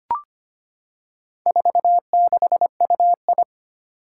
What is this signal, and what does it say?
Morse code, heard as a keyed tone, about 25 wpm, 700 Hz. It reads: 46UI
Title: Four Challenging Letter-Number Combinations 25wpm